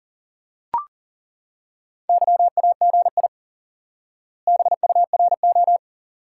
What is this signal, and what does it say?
Morse code, heard as a keyed tone, about 40 wpm, 700 Hz. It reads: YAGI BURO
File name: IQE Course 3 - Lesson 89 - YAGI - Two Elements - 40wpm